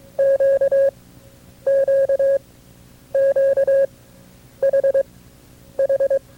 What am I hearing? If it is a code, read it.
Morse code, heard as a keyed tone, about 22 wpm, spaced as for 9 wpm, 550 Hz. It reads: QQQHH
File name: FreeFiles / CW-practice / KB6MT-tape2-side2